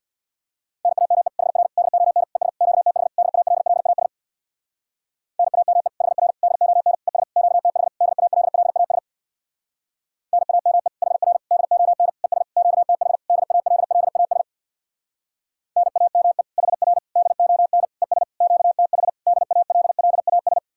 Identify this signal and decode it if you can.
Morse code, heard as a keyed tone, about 62 wpm, 700 Hz. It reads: NAME HR DON ES QTH DALLAS NAME HR DON ES QTH DALLAS NAME HR DON ES QTH DALLAS NAME HR DON ES QTH DALLAS
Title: Speed-Racing Common QSO Phrases Encore - 50wpm